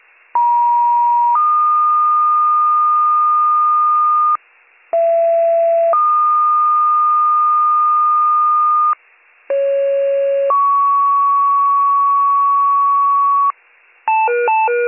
Two-tone page, heard as a one-second tone, then a three-second tone, about 950 then 1220 Hz, about 670 then 1220 Hz, about 550 then 1120 Hz. It is heard from 0.3 to 13.5 s.